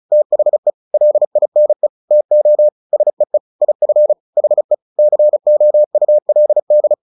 Morse code, heard as a keyed tone, about 34 wpm, 600 Hz. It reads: THE LINE TO SEE IF HE COULD
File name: the old man and the sea / the old man and the sea 1 35wpm